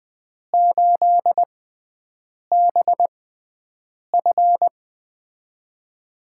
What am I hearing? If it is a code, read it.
Morse code, heard as a keyed tone, about 20 wpm, spaced as for 7 wpm, 700 Hz. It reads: 8BF
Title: USA Call Signs - 4 Characters - 6x Character Spacing - 20wpm